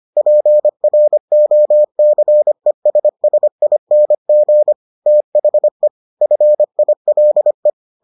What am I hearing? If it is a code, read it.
Morse code, heard as a keyed tone, about 25 wpm, 600 Hz. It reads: PROCESSING THE FILE